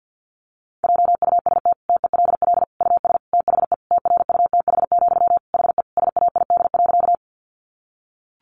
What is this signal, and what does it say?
Morse code, heard as a keyed tone, about 50 wpm, 700 Hz. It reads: JUST TELL US THE TRUTH, HE SAID.
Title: Sentences from Top 800 Words 50wpm